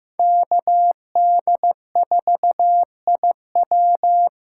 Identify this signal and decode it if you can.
Morse code, heard as a keyed tone, about 15 wpm, 700 Hz. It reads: KD4IW
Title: USA Call Signs - 5 Characters - 3x Character Spacing - 15wpm